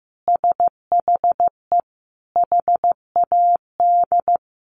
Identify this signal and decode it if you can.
Morse code, heard as a keyed tone, about 15 wpm, 700 Hz. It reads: SHE HAD
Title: Sentences from Top 900 Words 15wpm